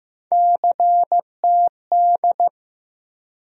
Morse code, heard as a keyed tone, about 15 wpm, 700 Hz. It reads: CTD
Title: USA Call Sign Postfixes - 3 Letters 15wpm